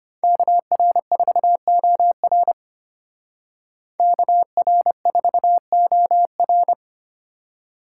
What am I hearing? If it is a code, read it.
Morse code, heard as a keyed tone, about 30 wpm, 700 Hz. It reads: KR4OR KR4OR
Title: Speed Racing - USA Call Signs 25wpm